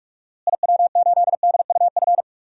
Morse code, heard as a keyed tone, about 46 wpm, 700 Hz. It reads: IW8DUF
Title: DXCC-100 Call Signs - 6 Characters - Repeated 2x - 45wpm